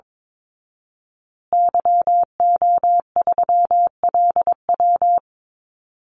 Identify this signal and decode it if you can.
Morse code, heard as a keyed tone, about 22 wpm, 700 Hz. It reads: YO3LW
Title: Call Signs Repeated 2x - 22wpm